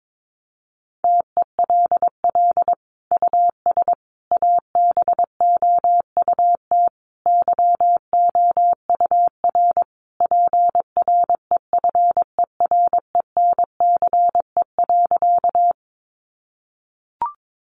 Morse code, heard as a keyed tone, about 22 wpm, 700 Hz. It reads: TELL US ABOUT YOUR PREFERENCE.
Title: Sentences from Top 1500 Words 22wpm